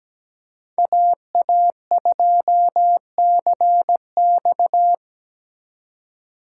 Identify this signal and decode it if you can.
Morse code, heard as a keyed tone, about 20 wpm, 700 Hz. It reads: AA2CX
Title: Speed Racing - USA 2x2 Call Signs 17wpm